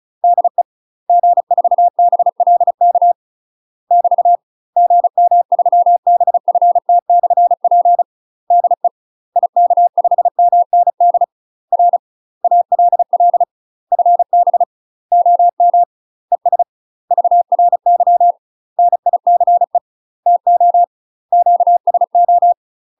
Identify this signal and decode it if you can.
Morse code, heard as a keyed tone, about 33 wpm, 700 Hz. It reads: DE G4BLK <BT> GM3BFT/P DE IK5MND R ALL FB OM ES VRY NICE TO QSO
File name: UK MASTER QSOs TRAINING / 35WPM / UK MASTER QSOs 0036